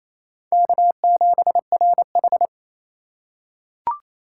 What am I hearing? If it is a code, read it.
Morse code, heard as a keyed tone, about 28 wpm, 700 Hz. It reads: K7RH
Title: USA Call Signs - 4 Characters - 4x Character Spacing - 28wpm